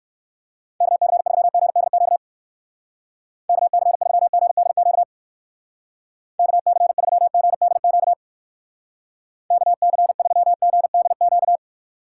Morse code, heard as a keyed tone, about 63 wpm, 700 Hz. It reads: KC3GDQ KC3GDQ KC3GDQ KC3GDQ